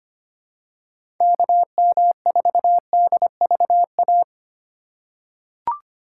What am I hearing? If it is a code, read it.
Morse code, heard as a keyed tone, about 25 wpm, 700 Hz. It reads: KM4DVA
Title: USA Call Signs - 6 Characters - 3x Character Spacing - 25wpm